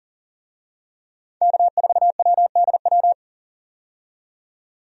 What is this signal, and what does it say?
Morse code, heard as a keyed tone, about 40 wpm, 700 Hz. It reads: K4WDW
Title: USA Call Signs - 5 Characters - 40wpm